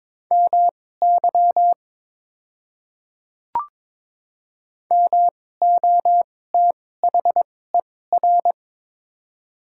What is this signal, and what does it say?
Morse code, heard as a keyed tone, about 22 wpm, spaced as for 16 wpm, 700 Hz. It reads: MY MOTHER
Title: 2x Character Spacing - Top 200 Words - 22wpm